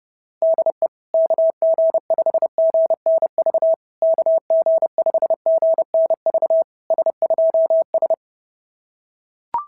Morse code, heard as a keyed tone, about 30 wpm, 660 Hz. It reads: DE KG5GNV KG5GNV S2S
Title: SOTA QSO Practice 30wpm